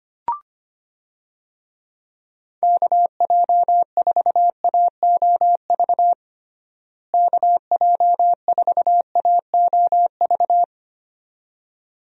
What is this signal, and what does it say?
Morse code, heard as a keyed tone, about 25 wpm, 700 Hz. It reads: KJ4AOV KJ4AOV
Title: USA Call Signs - 6 Characters - Repeated 2x - 25wpm